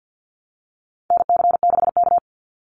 Morse code, heard as a keyed tone, about 50 wpm, 700 Hz. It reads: NC6K
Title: Call Signs - 50wpm